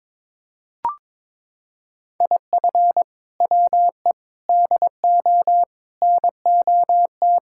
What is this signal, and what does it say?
Morse code, heard as a keyed tone, about 22 wpm, 700 Hz. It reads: IF WE DO NOT
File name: practice / 22wpm / Sentences with 4-Word Phrases - 22wpm